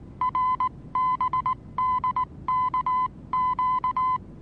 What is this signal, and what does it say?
Morse code, heard as a keyed tone, about 19 wpm, 1020 Hz. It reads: RBDKQ
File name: repos / Morsekurs / DJ4UF Morsen26